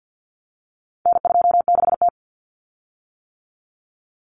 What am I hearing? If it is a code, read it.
Morse code, heard as a keyed tone, about 50 wpm, 700 Hz. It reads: N26T
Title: Common Cut Numbers - 4-Digit 50wpm